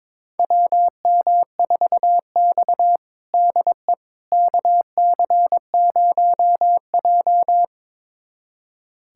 Morse code, heard as a keyed tone, about 21 wpm, 700 Hz. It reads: WM4X DE KC0J
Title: USA Callsign DE Callsign - 4 Characters - 22wpm